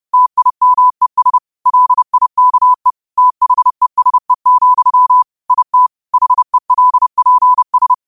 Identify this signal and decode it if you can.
Morse code, heard as a keyed tone, about 30 wpm, 1000 Hz. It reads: TIMES LIKE THESE, IT HELPS
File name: qotdfiles / 30wpm